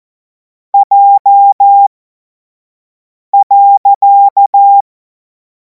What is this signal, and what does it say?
Morse code, heard as a keyed tone, about 14 wpm, spaced as for 5 wpm, 800 Hz. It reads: J.
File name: Morsekurs / Witze / Witz 02. Juni 2026 25 F70